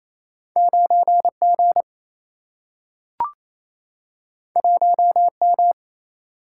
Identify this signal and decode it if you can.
Morse code, heard as a keyed tone, about 28 wpm, 700 Hz. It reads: 9G 1M
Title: Single-Letter Single-Number 28wpm